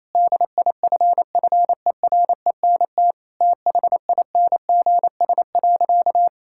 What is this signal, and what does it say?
Morse code, heard as a keyed tone, about 28 wpm, 700 Hz. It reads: DIFFERENT THINGS.
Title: Sentences from Top 1000 Words - Review Part 6 - 28wpm